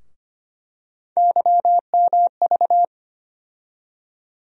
Morse code, heard as a keyed tone, about 25 wpm, 700 Hz. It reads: YMV
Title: USA Call Sign Postfixes - 3 Letters 25wpm